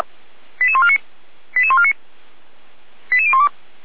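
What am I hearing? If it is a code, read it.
Five-tone selective call (ZVEI), heard as quick runs of five tones, tones of about 80 ms. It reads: 88149 88149 88012
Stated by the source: ZVEI five-tone selective call received by SDR.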